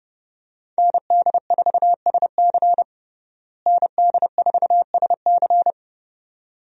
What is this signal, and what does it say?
Morse code, heard as a keyed tone, about 29 wpm, 700 Hz. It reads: ND4SC ND4SC
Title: USA Call Signs - 5 Characters - Repeated 2x - 30wpm